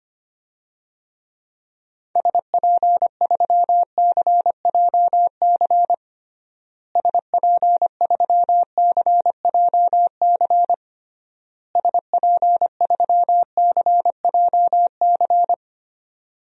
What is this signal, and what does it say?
Morse code, heard as a keyed tone, about 25 wpm, 700 Hz. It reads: SP3CJC SP3CJC SP3CJC
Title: DXCC-100 Call Signs - 6 Characters - Repeated 3x - 25wpm